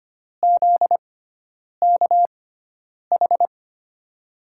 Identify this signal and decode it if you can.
Morse code, heard as a keyed tone, about 25 wpm, spaced as for 9 wpm, 700 Hz. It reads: ZKH
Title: USA Call Signs - 6x Character Spacing - 25wpm